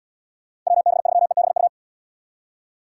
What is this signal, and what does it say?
Morse code, heard as a keyed tone, about 71 wpm, 700 Hz. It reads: WB3EBU